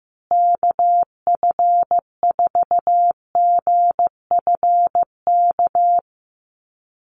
Morse code, heard as a keyed tone, about 15 wpm, 700 Hz. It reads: KF4GFK
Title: USA Call Signs Repeated 2x 15wpm